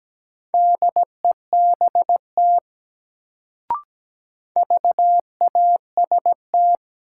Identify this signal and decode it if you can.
Morse code, heard as a keyed tone, about 17 wpm, 700 Hz. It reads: DEBT VAST
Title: Sentences from Top 1300 Words - New Word Drill - 17wpm